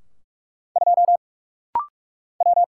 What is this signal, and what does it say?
Morse code, heard as a keyed tone, about 44 wpm, 700 Hz. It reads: 2 W
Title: Single Letter-Number Alt - Rapid-Fire 45wpm